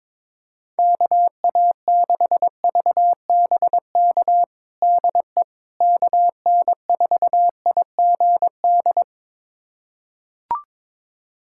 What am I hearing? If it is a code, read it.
Morse code, heard as a keyed tone, about 22 wpm, 700 Hz. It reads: KA6VBK DE KN4IGD
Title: USA Callsign DE Callsign 22wpm